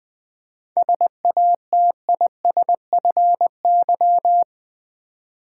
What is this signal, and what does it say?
Morse code, heard as a keyed tone, about 20 wpm, 700 Hz. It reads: SATISFY
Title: 7 Letter Words - First 3 Letters Given - 20wpm